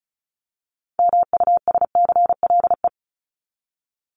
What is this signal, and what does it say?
Morse code, heard as a keyed tone, about 34 wpm, 700 Hz. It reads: MUSCLE
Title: Top 1251-1500 Words 35wpm